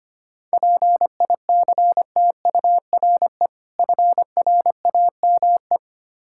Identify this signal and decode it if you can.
Morse code, heard as a keyed tone, about 25 wpm, 700 Hz. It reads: PICTURE FRAME
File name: practice / 25wpm / Difficult Words with Hints - 25wpm